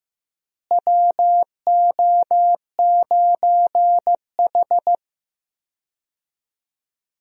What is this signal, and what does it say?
Morse code, heard as a keyed tone, about 15 wpm, 700 Hz. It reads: WO9H
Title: USA Call Signs - 4 Characters - Repeated 3x - 15wpm